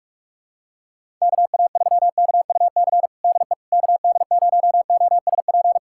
Morse code, heard as a keyed tone, about 46 wpm, 700 Hz. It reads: KA3KUC DE KD0OSP
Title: USA Callsign DE Callsign 45wpm